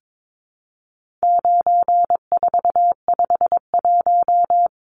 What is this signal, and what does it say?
Morse code, heard as a keyed tone, about 22 wpm, 700 Hz. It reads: 9451
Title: Numbers - 4-Digit 22wpm